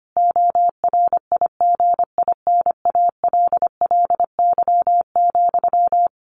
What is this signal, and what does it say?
Morse code, heard as a keyed tone, about 25 wpm, 700 Hz. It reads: ORIGINALLY,
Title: A Sound Of Thunder 25wpm